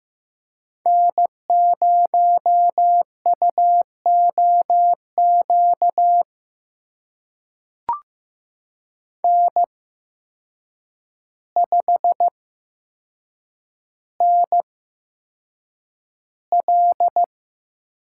Morse code, heard as a keyed tone, about 15 wpm, spaced as for 7 wpm, 700 Hz. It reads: N0UOQ N5NL
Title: USA Call Signs - 8x Character Spacing - 15wpm